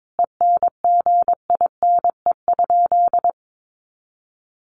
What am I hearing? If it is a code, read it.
Morse code, heard as a keyed tone, about 22 wpm, 700 Hz. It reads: ENGINE?
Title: Sentences from Top 1500 Words 22wpm